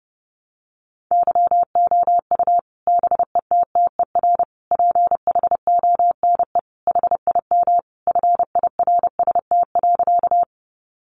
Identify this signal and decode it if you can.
Morse code, heard as a keyed tone, about 30 wpm, 700 Hz. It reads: YOU BETTER PHONE HIM FIRST.
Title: Sentences from Top 1100 Words 30wpm